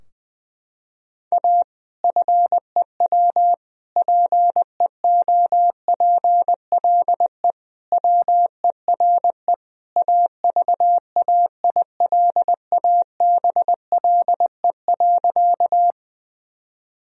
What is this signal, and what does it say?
Morse code, heard as a keyed tone, about 21 wpm, 700 Hz. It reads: A FEW PEOPLE WERE AVAILABLE.
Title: Sentences from Top 1000 Words - Review Part 3 - 20wpm